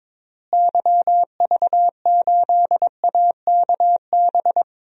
Morse code, heard as a keyed tone, about 22 wpm, 700 Hz. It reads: YV8AKB
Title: DXCC-100 Call Signs - 6 Characters - Repeated 3x - 22wpm